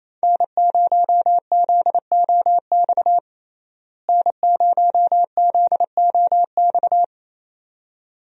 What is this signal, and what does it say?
Morse code, heard as a keyed tone, about 28 wpm, 700 Hz. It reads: N0ZOX N0ZOX
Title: USA Call Signs - 5 Characters - Repeated 2x - 28wpm